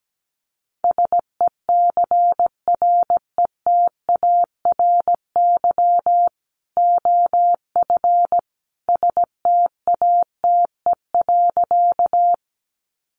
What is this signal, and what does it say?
Morse code, heard as a keyed tone, about 17 wpm, 700 Hz. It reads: SECRETARY OF STATE.
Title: Sentences from Top 1300 Words 17wpm